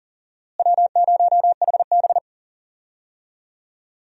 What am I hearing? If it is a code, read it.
Morse code, heard as a keyed tone, about 40 wpm, 700 Hz. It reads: W0HB
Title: USA Call Signs - 4 Characters - Repeated 3x - 40wpm